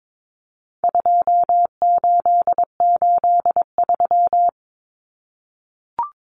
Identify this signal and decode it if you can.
Morse code, heard as a keyed tone, about 22 wpm, 700 Hz. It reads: 2883
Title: Common Cut Numbers - 4-Digit 22wpm